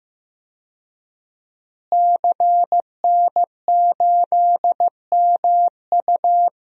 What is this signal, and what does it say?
Morse code, heard as a keyed tone, about 15 wpm, 700 Hz. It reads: CN8MU
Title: DXCC-100 Call Signs - 5 Characters - Repeated 3x - 15wpm